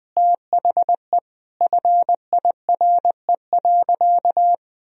Morse code, heard as a keyed tone, about 21 wpm, 700 Hz. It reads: THE FIRE.
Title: Sentences from Top 1000 Words - Review Part 1 - 20wpm